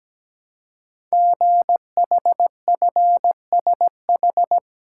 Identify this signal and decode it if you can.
Morse code, heard as a keyed tone, about 17 wpm, 700 Hz. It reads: GHFSH
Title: Code Groups - 5 Letters - 17wpm